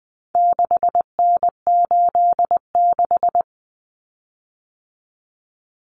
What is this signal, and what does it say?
Morse code, heard as a keyed tone, about 20 wpm, 700 Hz. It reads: 6N86
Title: Common Cut Numbers - 4-Digit 20wpm